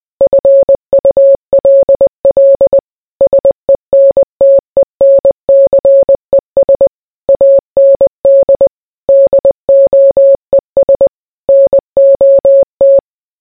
Morse code, heard as a keyed tone, about 20 wpm, 550 Hz. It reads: FULL SENTENCES AND DOES NOT